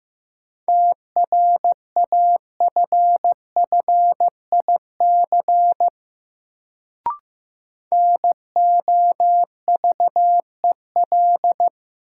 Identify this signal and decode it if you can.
Morse code, heard as a keyed tone, about 15 wpm, 700 Hz. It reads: TRAFFIC NOVEL
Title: Sentences from Top 1300 Words - New Word Drill - 15wpm